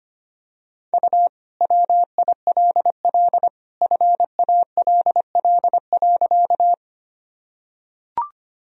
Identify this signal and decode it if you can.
Morse code, heard as a keyed tone, about 25 wpm, 700 Hz. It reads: U WILL FALL.
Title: Sentences from Top 500 Words and Abbreviations 25wpm